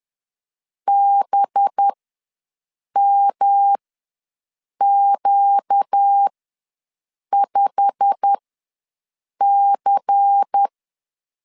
Morse code, heard as a keyed tone, about 10 wpm, spaced as for 6 wpm, 800 Hz. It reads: BMQ5C
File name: LEZIONI / 10 SIGNS / 10D COMA ws 10wpm 4314